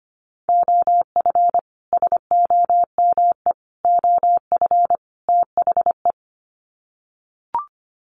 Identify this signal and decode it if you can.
Morse code, heard as a keyed tone, about 25 wpm, 700 Hz. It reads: OF SOME OF THE
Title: Sets of 4 Words 25wpm